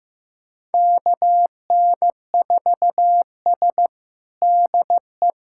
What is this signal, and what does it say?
Morse code, heard as a keyed tone, about 16 wpm, 700 Hz. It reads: KN4S DE
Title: USA Callsign DE Callsign - 4 Characters - 15wpm